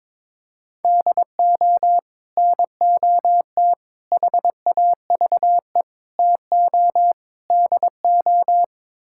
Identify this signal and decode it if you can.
Morse code, heard as a keyed tone, about 22 wpm, 700 Hz. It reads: DO NOT HAVE TO DO
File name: practice / 22wpm / Build A Phrase - 5-Word Phrases - 22wpm